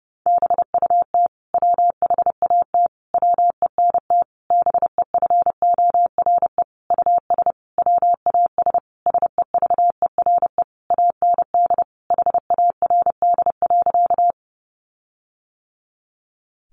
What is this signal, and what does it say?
Morse code, heard as a keyed tone, about 30 wpm, 700 Hz. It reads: BUT WHAT WENT BEFORE US WAS SEVERE AND HARD.
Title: Sentences from Top 1400 Words 30wpm